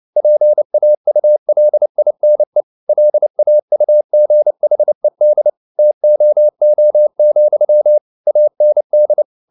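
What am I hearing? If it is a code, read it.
Morse code, heard as a keyed tone, about 29 wpm, 600 Hz. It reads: PAULINE LAUGHED TOO, AND
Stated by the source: Morse code at 29 wpm, sending Ken Follett's novel Never in English, from CW med Gnister.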